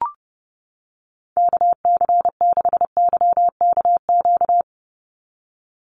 Morse code, heard as a keyed tone, about 30 wpm, 700 Hz. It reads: KC6YKQ
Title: USA Call Signs 30wpm